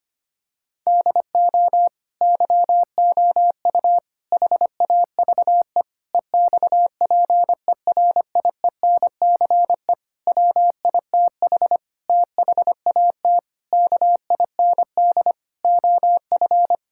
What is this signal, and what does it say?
Morse code, heard as a keyed tone, about 25 wpm, 700 Hz. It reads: DO YOU HAVE EXPERIENCE WITH THAT KIND OF
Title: Sentences from Top 1000 Words - Review Part 5 - 25wpm